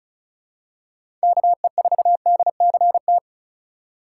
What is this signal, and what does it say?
Morse code, heard as a keyed tone, about 34 wpm, 700 Hz. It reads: KE4DCT
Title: USA Call Signs - 6 Characters - 3x Character Spacing - 35wpm